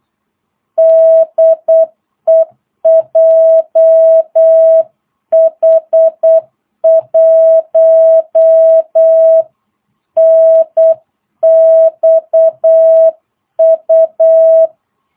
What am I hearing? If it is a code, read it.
Morse code, heard as a keyed tone, about 8 wpm, 660 Hz. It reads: DE JH1NXU
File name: jh8cbh / 2022cw